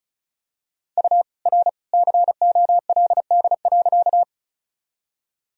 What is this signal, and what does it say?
Morse code, heard as a keyed tone, about 36 wpm, 700 Hz. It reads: U R COLD.
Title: Sentences from Top 400 Words and Abbreviations 35wpm